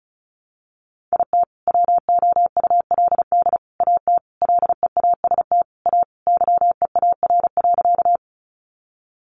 Morse code, heard as a keyed tone, about 34 wpm, 700 Hz. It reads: IT WOULD AT LEAST A YEAR.
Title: Sentences from Top 700 Words 35wpm